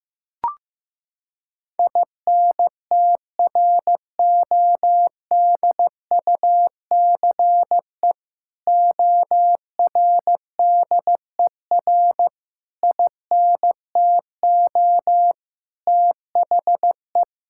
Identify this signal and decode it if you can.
Morse code, heard as a keyed tone, about 15 wpm, 700 Hz. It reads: INTRODUCE ORDER INTO THE
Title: Sentences from Top 1600 Words 15wpm